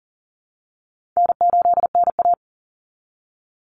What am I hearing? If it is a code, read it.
Morse code, heard as a keyed tone, about 40 wpm, 700 Hz. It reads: N8NA
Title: Easy Call Signs - 40wpm